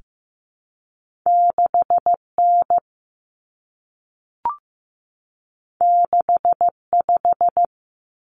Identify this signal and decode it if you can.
Morse code, heard as a keyed tone, about 15 wpm, 700 Hz. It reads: 6N 65
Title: Common Cut Numbers - 2-Digit 15wpm